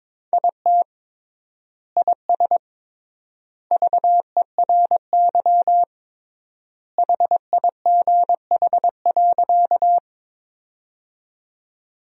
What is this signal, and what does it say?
Morse code, heard as a keyed tone, about 22 wpm, 700 Hz. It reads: IT IS VERY HIGH.
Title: 3x Word-Spacing - Sentences from Top 200 Words - 22wpm